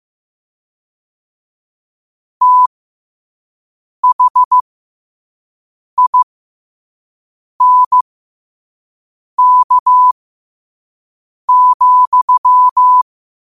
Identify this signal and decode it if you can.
Morse code, heard as a keyed tone, about 15 wpm, spaced as for 5 wpm, 1000 Hz. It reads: THINK,
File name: qotdfiles / 5wpm